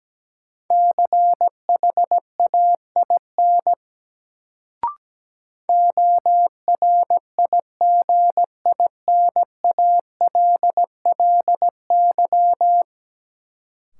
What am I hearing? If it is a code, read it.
Morse code, heard as a keyed tone, about 17 wpm, 700 Hz. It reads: CHAIN ORIGINALLY
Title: Sentences from Top 1500 Words - New Word Drill - 17wpm